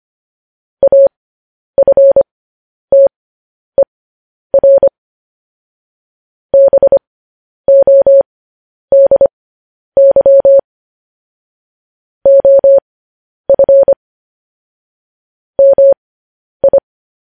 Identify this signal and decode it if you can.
Morse code, heard as a keyed tone, about 25 wpm, spaced as for 10 wpm, 550 Hz. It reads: AFTER BODY OF MI
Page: news headlines in morse code at 10 wpm